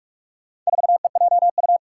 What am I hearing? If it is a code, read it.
Morse code, heard as a keyed tone, about 46 wpm, 700 Hz. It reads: 4EJU
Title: Four Challenging Letter-Number Combinations 45wpm